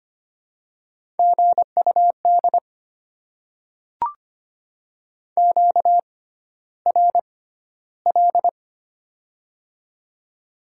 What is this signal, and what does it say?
Morse code, heard as a keyed tone, about 25 wpm, spaced as for 13 wpm, 700 Hz. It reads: GUD QRL
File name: Top 20 QSO Elements - 6x Character Spacing - 25wpm